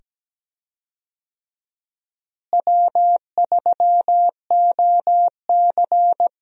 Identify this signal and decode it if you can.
Morse code, heard as a keyed tone, about 17 wpm, 700 Hz. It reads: W3OC
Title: USA Call Signs - 4 Characters - Repeated 2x - 17wpm